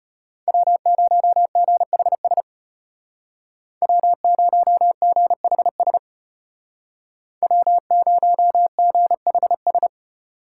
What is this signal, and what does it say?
Morse code, heard as a keyed tone, about 32 wpm, 700 Hz. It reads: W0GHS W0GHS W0GHS